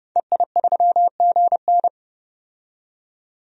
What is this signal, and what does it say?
Morse code, heard as a keyed tone, about 30 wpm, 700 Hz. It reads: EI3GN
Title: DXCC-100 Call Signs - 5 Characters - 30wpm